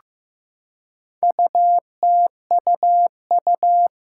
Morse code, heard as a keyed tone, about 15 wpm, 700 Hz. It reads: UTUU
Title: Four Challenging Letter-Number Combinations 15wpm